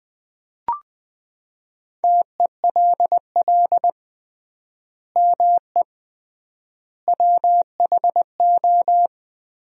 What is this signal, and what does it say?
Morse code, heard as a keyed tone, about 20 wpm, 700 Hz. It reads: TELL ME WHO
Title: 3x Word-Spacing - Sentences from Top 200 Words - 20wpm